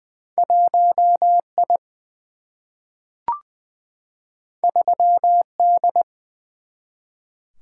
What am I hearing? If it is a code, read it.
Morse code, heard as a keyed tone, about 20 wpm, 700 Hz. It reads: 1I 3D
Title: Single-Letter Single-Number 20wpm